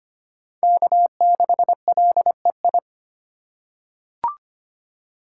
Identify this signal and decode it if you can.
Morse code, heard as a keyed tone, about 25 wpm, 700 Hz. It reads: K6LEI
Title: USA Call Signs - 2x Character Spacing - 25wpm